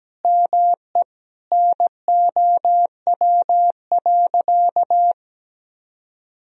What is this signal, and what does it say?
Morse code, heard as a keyed tone, about 17 wpm, 700 Hz. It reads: ME NOW.
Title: Sentences from Top 1000 Words - Review Part 2 - 17wpm